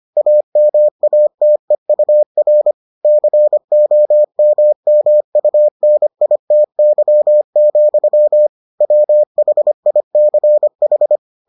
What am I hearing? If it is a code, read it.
Morse code, heard as a keyed tone, about 25 wpm, 600 Hz. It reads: AMATEUR COMMUNITY, WHICH